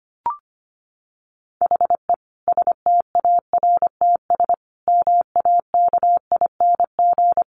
Morse code, heard as a keyed tone, about 25 wpm, 700 Hz. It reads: HE STARTS MAKING
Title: Additional Easy Phrases 25wpm